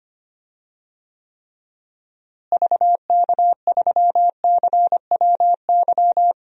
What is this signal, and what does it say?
Morse code, heard as a keyed tone, about 25 wpm, 700 Hz. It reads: VK3CWY
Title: DXCC-100 Call Signs - 6 Characters - Repeated 2x - 25wpm